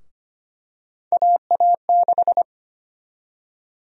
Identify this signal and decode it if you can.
Morse code, heard as a keyed tone, about 25 wpm, 700 Hz. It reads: AA6
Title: USA Call Sign Prefixes - 25wpm